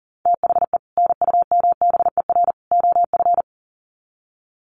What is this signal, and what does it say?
Morse code, heard as a keyed tone, about 40 wpm, 700 Hz. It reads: THE NUMBER OF
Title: Sets of 3 Words 40wpm